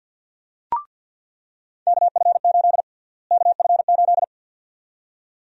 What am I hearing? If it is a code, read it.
Morse code, heard as a keyed tone, about 49 wpm, 700 Hz. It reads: KF8 KF8
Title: USA Call Sign Prefixes - Repeated 2x - 50wpm